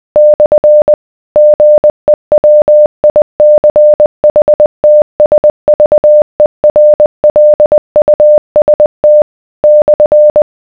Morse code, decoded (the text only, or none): / GEWICHTSVERLUST /